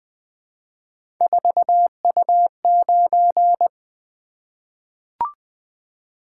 Morse code, heard as a keyed tone, about 20 wpm, 700 Hz. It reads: 4U9
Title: Three Letter-Number Combinations - 20wpm